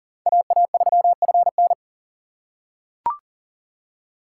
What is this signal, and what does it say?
Morse code, heard as a keyed tone, about 40 wpm, 700 Hz. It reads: AA3FN
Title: USA Call Signs - 5 Characters - 6x Character Spacing - 40wpm